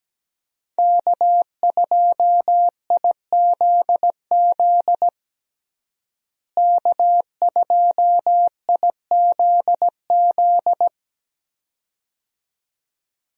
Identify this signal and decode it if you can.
Morse code, heard as a keyed tone, about 17 wpm, 700 Hz. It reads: K2IZZ K2IZZ
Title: USA Call Signs - 5 Characters - Repeated 2x - 17wpm